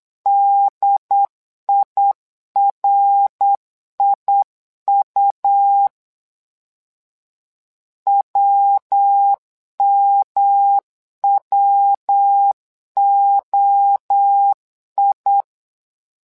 Morse code, heard as a keyed tone, about 8 wpm, 800 Hz. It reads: DIRIU WMWOI
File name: LEZIONI / 3 MWRO / 3 MWRO 8wpm 2606